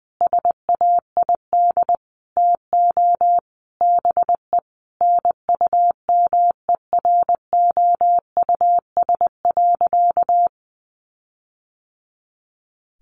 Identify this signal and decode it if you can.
Morse code, heard as a keyed tone, about 20 wpm, 700 Hz. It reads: SAID TO BE NUMEROUS.
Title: Sentences from Top 1400 Words 20wpm